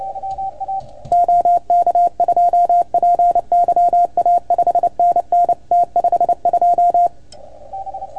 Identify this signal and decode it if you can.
Morse code, heard as a keyed tone, about 28 wpm, 690 Hz. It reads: OK2PYA 5NN T52
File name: Archiv / CUC310 / OK1FGD CUC 310 pileup